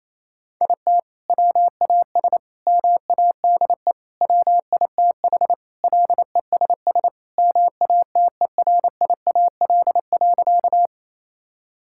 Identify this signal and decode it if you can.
Morse code, heard as a keyed tone, about 28 wpm, 700 Hz. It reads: IT WAS MADE WITH LESS MATERIAL.
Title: Sentences from Top 1000 Words - Review Part 5 - 28wpm